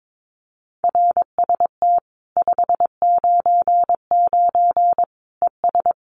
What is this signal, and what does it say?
Morse code, heard as a keyed tone, about 22 wpm, 700 Hz. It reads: RST 599 ES
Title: Common QSO Phrases 22wpm